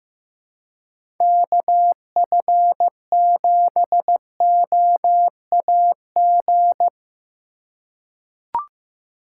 Morse code, heard as a keyed tone, about 15 wpm, 700 Hz. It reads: KF7OAG
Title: USA Call Signs - 6 Characters - 3x Character Spacing - 15wpm